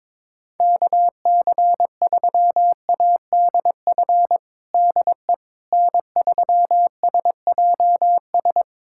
Morse code, heard as a keyed tone, about 21 wpm, 700 Hz. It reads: KC3ADF DE N3SJS
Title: USA Callsign DE Callsign 22wpm